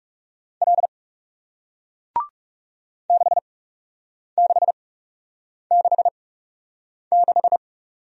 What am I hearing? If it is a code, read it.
Morse code, heard as a keyed tone, about 41 wpm, spaced as for 8 wpm, 700 Hz. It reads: L 6666